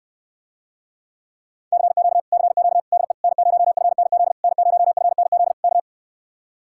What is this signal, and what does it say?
Morse code, heard as a keyed tone, about 70 wpm, 700 Hz. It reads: CQ CQ DE N0FNZ N0FNZ K